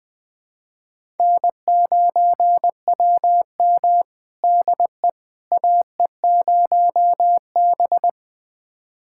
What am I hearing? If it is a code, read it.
Morse code, heard as a keyed tone, about 20 wpm, 700 Hz. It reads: N9WM DE AE0B
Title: USA Callsign DE Callsign - 4 Characters - 20wpm